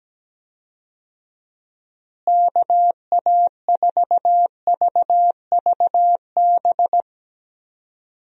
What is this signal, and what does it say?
Morse code, heard as a keyed tone, about 17 wpm, 700 Hz. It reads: KA4VVB
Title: USA Call Signs - 6 Characters - Repeated 2x - 17wpm